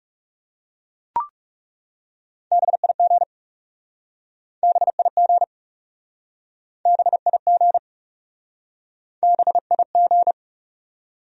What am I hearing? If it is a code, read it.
Morse code, heard as a keyed tone, about 40 wpm, 700 Hz. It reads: BIG BIG BIG BIG